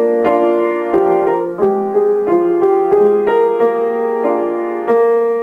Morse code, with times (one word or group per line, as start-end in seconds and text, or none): none